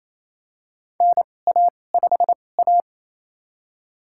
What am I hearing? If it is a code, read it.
Morse code, heard as a keyed tone, about 28 wpm, spaced as for 20 wpm, 700 Hz. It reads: NA5A
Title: USA Call Signs - 4 Characters - 2x Character Spacing - 28wpm